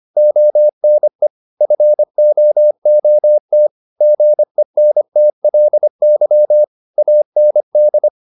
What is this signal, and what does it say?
Morse code, heard as a keyed tone, about 25 wpm, 600 Hz. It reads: ONE FOOT GENTLY AND
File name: the old man and the sea part 1 25wpm